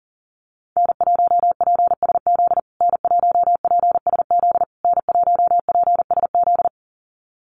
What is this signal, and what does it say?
Morse code, heard as a keyed tone, about 40 wpm, 700 Hz. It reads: N1PSZ N1PSZ N1PSZ
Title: USA Call Signs Repeated 3x 40wpm